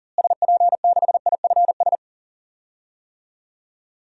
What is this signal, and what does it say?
Morse code, heard as a keyed tone, about 40 wpm, 700 Hz. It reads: SP6IFS
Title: DXCC-100 Call Signs - 6 Characters - Repeated 2x - 40wpm